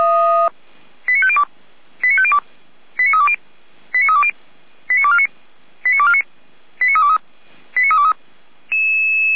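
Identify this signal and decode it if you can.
Five-tone selective call (ZVEI), heard as quick runs of five tones, tones of about 70 ms. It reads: 89502 89502 89320 89320 89249 89249 89323 89323
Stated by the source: ZVEI five-tone selective call received via SDR, from the Feuerwehr Orlamünde audio archive.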